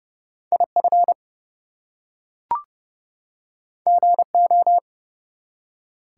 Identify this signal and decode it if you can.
Morse code, heard as a keyed tone, about 31 wpm, 700 Hz. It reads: IF GO
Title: Two Letter Words - 30wpm